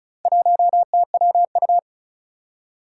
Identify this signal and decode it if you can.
Morse code, heard as a keyed tone, about 36 wpm, 700 Hz. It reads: 1TWU
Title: Four Challenging Letter-Number Combinations 35wpm